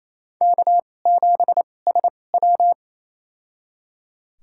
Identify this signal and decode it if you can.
Morse code, heard as a keyed tone, about 28 wpm, spaced as for 20 wpm, 700 Hz. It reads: K7SW
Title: USA Call Signs - 4 Characters - 2x Character Spacing - 28wpm